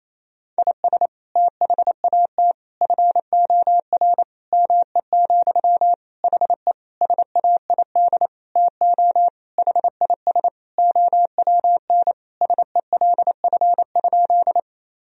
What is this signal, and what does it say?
Morse code, heard as a keyed tone, about 27 wpm, 700 Hz. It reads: IS THAT FOR ME, HE SAID TO HIS OWN SELF?
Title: Sentences from Top 1000 Words - Review Part 1 - 28wpm